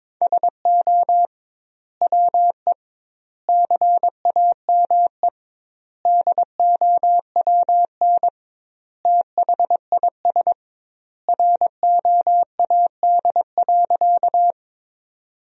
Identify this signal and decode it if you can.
Morse code, heard as a keyed tone, about 22 wpm, 700 Hz. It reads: SO WE CAME DOWN THIS ROAD.
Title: 2x Word-Spacing - Sentences from Top 400 Words - 22wpm